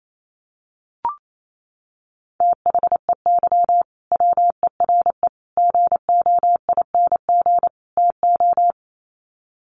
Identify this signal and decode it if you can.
Morse code, heard as a keyed tone, about 28 wpm, 700 Hz. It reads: THEY WERE GOING TO
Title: Sets of 4 Words 28wpm